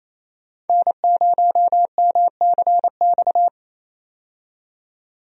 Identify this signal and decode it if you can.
Morse code, heard as a keyed tone, about 28 wpm, 700 Hz. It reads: N0MCX
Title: USA Call Signs - 5 Characters - 28wpm